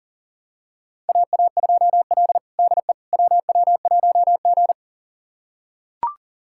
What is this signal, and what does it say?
Morse code, heard as a keyed tone, about 40 wpm, 700 Hz. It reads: AA2L DE WW1G
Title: USA Callsign DE Callsign - 4 Characters - 40wpm